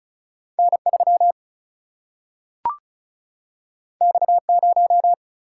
Morse code, heard as a keyed tone, about 36 wpm, 700 Hz. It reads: N 3 X 0
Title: Single-Letter Single-Number 35wpm